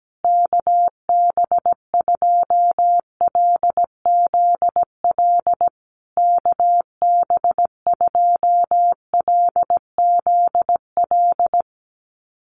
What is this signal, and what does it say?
Morse code, heard as a keyed tone, about 17 wpm, 700 Hz. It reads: KB2LZL KB2LZL
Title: USA Call Signs Repeated 3x 17wpm